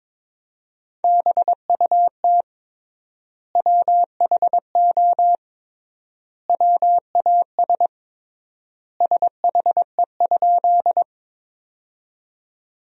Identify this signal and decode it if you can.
Morse code, heard as a keyed tone, about 22 wpm, 700 Hz. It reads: BUT WHO WAS SHE?
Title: 3x Word-Spacing - Sentences from Top 100 Words - 22wpm